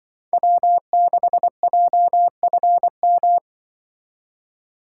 Morse code, heard as a keyed tone, about 24 wpm, 700 Hz. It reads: W6JFM
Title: Speed Racing - USA 1x3 Call Signs 20wpm